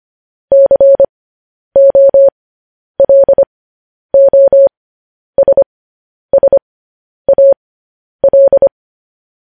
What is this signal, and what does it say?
Morse code, heard as a keyed tone, about 25 wpm, spaced as for 10 wpm, 550 Hz. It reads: COLOSSAL